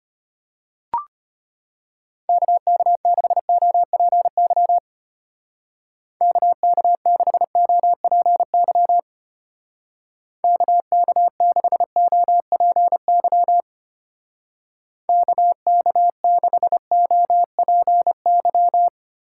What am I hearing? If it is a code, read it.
Morse code, heard as a keyed tone, about 30 wpm, 700 Hz. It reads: KK6OPY KK6OPY KK6OPY KK6OPY
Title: Speed Racing - USA 2x3 Call Signs 25wpm